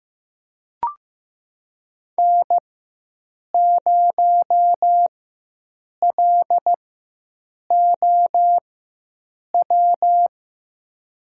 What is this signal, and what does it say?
Morse code, heard as a keyed tone, about 15 wpm, spaced as for 7 wpm, 700 Hz. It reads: N0LOW
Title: USA Call Signs - 5 Characters - 4x Character Spacing - 15wpm